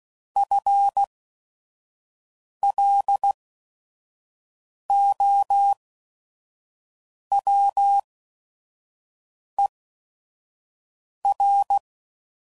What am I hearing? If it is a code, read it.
Morse code, heard as a keyed tone, about 16 wpm, spaced as for 5 wpm, 800 Hz. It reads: FLOWER
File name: Prefixes / FLs & THs / FL Runs